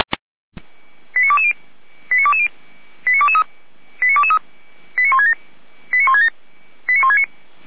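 Five-tone selective call (ZVEI), heard as quick runs of five tones, tones of about 70 ms. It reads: 89220 89220 89223 89223 89167 89167 89169
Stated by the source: ZVEI five-tone selective call received on an SDR.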